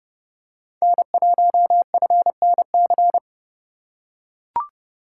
Morse code, heard as a keyed tone, about 30 wpm, 700 Hz. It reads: N1FNC
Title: USA Call Signs - 5 Characters - 4x Character Spacing - 30wpm